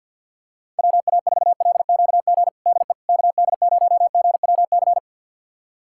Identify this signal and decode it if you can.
Morse code, heard as a keyed tone, about 48 wpm, 700 Hz. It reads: WA4LQG DE KD0GWC
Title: USA Callsign DE Callsign - 6 Characters - 50wpm